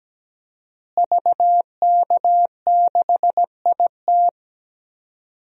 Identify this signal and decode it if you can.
Morse code, heard as a keyed tone, about 17 wpm, 700 Hz. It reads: VK6IT
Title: DXCC-100 Call Signs - 5 Characters - Repeated 3x - 17wpm